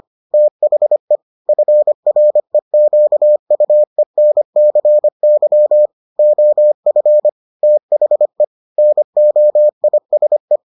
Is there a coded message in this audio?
Yes: Morse code from 0.3 to 10.6 s, heard as a keyed tone, about 25 wpm, 600 Hz. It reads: THE FREQUENCY OF THE NOISE